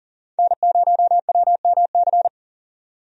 Morse code, heard as a keyed tone, about 40 wpm, 700 Hz. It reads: N0WMC
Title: USA Call Signs - 5 Characters - 40wpm